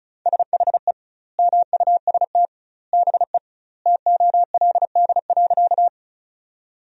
Morse code, heard as a keyed tone, about 36 wpm, 700 Hz. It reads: SHE MUST BE TOLD.
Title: 2x Word-Spacing - Sentences from Top 400 Words - 35wpm